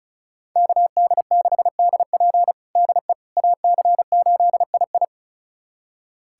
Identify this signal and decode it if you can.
Morse code, heard as a keyed tone, about 34 wpm, 700 Hz. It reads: KD6DP DE AC8II
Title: USA Callsign DE Callsign - 5 Characters - 35wpm